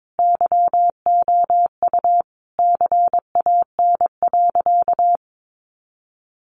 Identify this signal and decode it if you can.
Morse code, heard as a keyed tone, about 22 wpm, 700 Hz. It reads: YOU CAN.
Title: Sentences from Top 1500 Words 22wpm